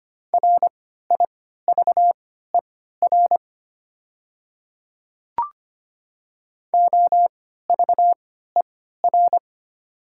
Morse code, heard as a keyed tone, about 25 wpm, spaced as for 14 wpm, 700 Hz. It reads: RIVER OVER
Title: 3x Character Spacing - Top 500 Words - 25wpm